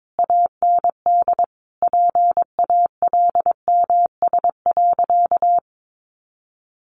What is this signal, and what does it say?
Morse code, heard as a keyed tone, about 22 wpm, 700 Hz. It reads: AND PALMS.
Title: A Sound Of Thunder 22wpm